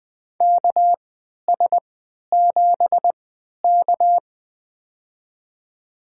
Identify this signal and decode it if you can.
Morse code, heard as a keyed tone, about 20 wpm, spaced as for 11 wpm, 700 Hz. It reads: KS7K
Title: USA Call Signs - 4 Characters - 3x Character Spacing - 20wpm